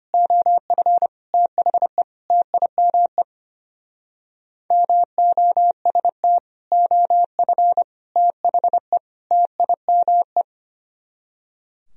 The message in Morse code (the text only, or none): OF THE TIME MOST OF THE TIME